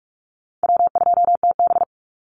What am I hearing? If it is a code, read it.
Morse code, heard as a keyed tone, about 46 wpm, 700 Hz. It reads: W2TB
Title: Call Signs - 45wpm